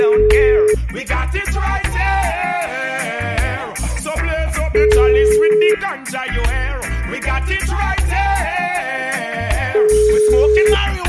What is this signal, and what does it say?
Ringback tone (Europe, 425 Hz), with speech.